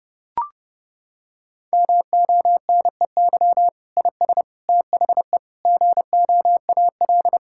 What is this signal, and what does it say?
Morse code, heard as a keyed tone, about 30 wpm, 700 Hz. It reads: MONEY IS THE GOAL
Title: Sentences from Top 1000 Words - Review Part 5 - 30wpm